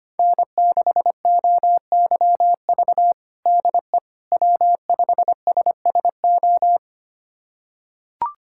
Morse code, heard as a keyed tone, about 25 wpm, 700 Hz. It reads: N6OYV DE W5SSO
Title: USA Callsign DE Callsign - 5 Characters - 25wpm